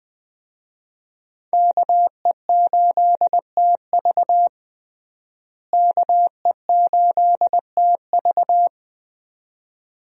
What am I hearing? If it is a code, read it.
Morse code, heard as a keyed tone, about 20 wpm, 700 Hz. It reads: KE8TV KE8TV
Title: USA Call Signs - 5 Characters - Repeated 2x - 20wpm